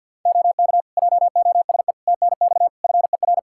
Morse code, heard as a keyed tone, about 51 wpm, 700 Hz. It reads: OK JOSE TNX FER
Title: K1USN SST QSO Practice 50wpm